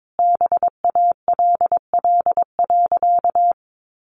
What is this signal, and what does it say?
Morse code, heard as a keyed tone, about 22 wpm, 700 Hz. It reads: BALL.
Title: Easy Phrases 22wpm